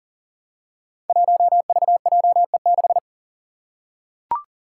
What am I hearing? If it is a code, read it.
Morse code, heard as a keyed tone, about 40 wpm, 700 Hz. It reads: 1VJE6
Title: Five Challenging Letter-Number Combinations 40wpm